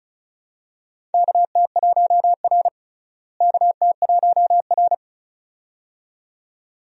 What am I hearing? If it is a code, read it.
Morse code, heard as a keyed tone, about 36 wpm, 700 Hz. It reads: KT1R KT1R
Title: USA Call Signs - 4 Characters - Repeated 2x - 35wpm